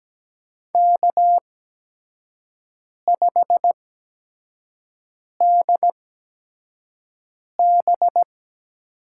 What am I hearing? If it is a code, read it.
Morse code, heard as a keyed tone, about 17 wpm, spaced as for 5 wpm, 700 Hz. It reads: K5DB
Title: USA Call Signs - 8x Character Spacing - 17wpm